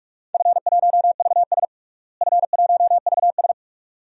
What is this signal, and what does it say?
Morse code, heard as a keyed tone, about 46 wpm, 700 Hz. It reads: F1VS F1VS
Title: DXCC-100 Call Signs - 4 Characters - Repeated 2x - 45wpm